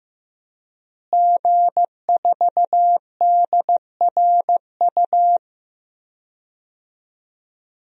Morse code, heard as a keyed tone, about 16 wpm, 700 Hz. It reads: G4DRU
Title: DXCC-100 Call Signs - 5 Characters - Repeated 3x - 15wpm